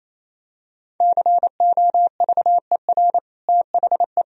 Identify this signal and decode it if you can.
Morse code, heard as a keyed tone, about 28 wpm, 700 Hz. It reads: COVER THE
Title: Sentences from Top 1000 Words - Review Part 1 - 28wpm